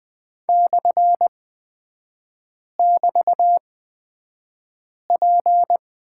Morse code, heard as a keyed tone, about 20 wpm, spaced as for 5 wpm, 700 Hz. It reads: /<BT>P
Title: 224 Lesson 36e - BT - Three Characters 20wpm